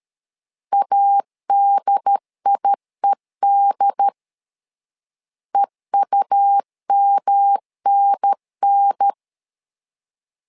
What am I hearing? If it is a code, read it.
Morse code, heard as a keyed tone, about 13 wpm, 800 Hz. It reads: ADIED EUMNN
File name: LEZIONI / 3 MWRO / 3 MW 12wpm 1633